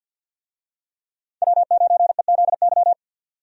Morse code, heard as a keyed tone, about 51 wpm, 700 Hz. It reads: W9EZY
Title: USA Call Signs - 5 Characters - 50wpm